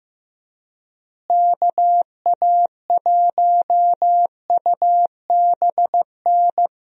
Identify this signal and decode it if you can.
Morse code, heard as a keyed tone, about 15 wpm, 700 Hz. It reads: KA1UBN
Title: USA Call Signs - 6 Characters - Repeated 2x - 15wpm